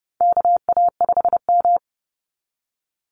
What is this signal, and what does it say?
Morse code, heard as a keyed tone, about 30 wpm, 700 Hz. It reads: KA5M
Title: Call Signs - 30wpm